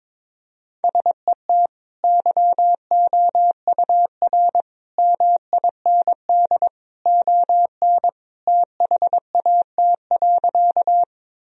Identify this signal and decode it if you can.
Morse code, heard as a keyed tone, about 21 wpm, 700 Hz. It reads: SET YOUR MIND ON THAT.
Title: Sentences from Top 1000 Words - Review Part 1 - 22wpm